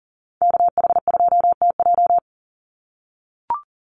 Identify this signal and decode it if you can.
Morse code, heard as a keyed tone, about 40 wpm, 700 Hz. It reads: KH2TJ
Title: NA SOTA Chaser Callsigns 40wpm